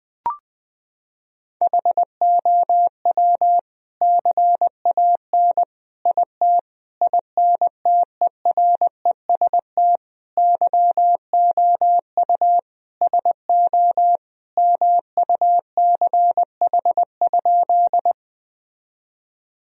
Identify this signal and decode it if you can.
Morse code, heard as a keyed tone, about 20 wpm, 700 Hz. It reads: HOW CAN IT INTEREST YOU SO MUCH?
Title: Sentences from Top 1000 Words - Review Part 2 - 20wpm